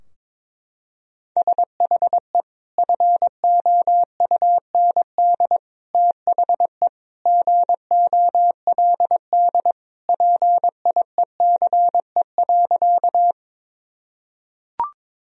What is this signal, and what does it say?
Morse code, heard as a keyed tone, about 22 wpm, 700 Hz. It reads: SHE FOUND THE GOLD PIECE.
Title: Sentences from Top 1000 Words - Review Part 3 - 22wpm